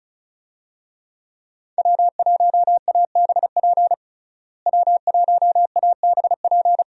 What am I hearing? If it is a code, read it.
Morse code, heard as a keyed tone, about 34 wpm, 700 Hz. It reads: W1ABP W1ABP
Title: USA Call Signs - 5 Characters - Repeated 2x - 35wpm